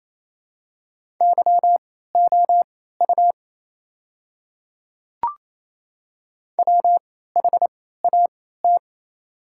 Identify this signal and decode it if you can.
Morse code, heard as a keyed tone, about 28 wpm, spaced as for 16 wpm, 700 Hz. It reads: YOU WHAT
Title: 3x Character Spacing - Top 100 Words - 28wpm